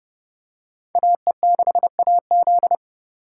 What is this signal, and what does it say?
Morse code, heard as a keyed tone, about 31 wpm, 700 Hz. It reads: AE6AZ